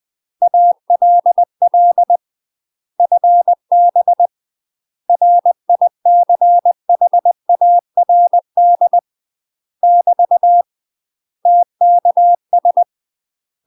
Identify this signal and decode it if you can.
Morse code, heard as a keyed tone, about 20 wpm, 700 Hz. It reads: ALL FB RICHARD <BT> TKS